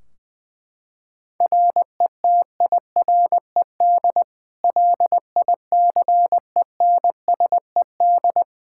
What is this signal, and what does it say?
Morse code, heard as a keyed tone, about 20 wpm, 700 Hz. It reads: RETIRED LICENSED
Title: Practice QSO - Exchange 3 - More Info - 20wpm